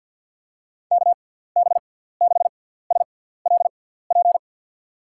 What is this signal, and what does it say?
Morse code, heard as a keyed tone, about 52 wpm, spaced as for 17 wpm, 700 Hz. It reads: KB6SLP
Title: USA Call Signs - 6 Characters - 6x Character Spacing - 50wpm